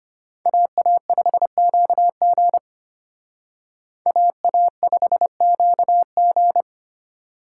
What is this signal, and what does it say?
Morse code, heard as a keyed tone, about 32 wpm, 700 Hz. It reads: AA5QG AA5QG